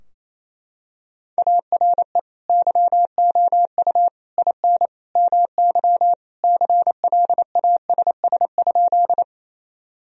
Morse code, heard as a keyed tone, about 28 wpm, 700 Hz. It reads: ARE YOU IN MY CLASS?
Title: Sentences from Top 1000 Words - Review Part 2 - 28wpm